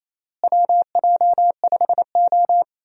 Morse code, heard as a keyed tone, about 28 wpm, 700 Hz. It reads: WJ5O
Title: USA Call Signs - 4 Characters - Repeated 3x - 28wpm